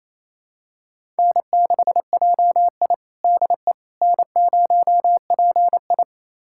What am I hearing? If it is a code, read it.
Morse code, heard as a keyed tone, about 28 wpm, 700 Hz. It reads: N6JI DE N0PI
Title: USA Callsign DE Callsign - 4 Characters - 28wpm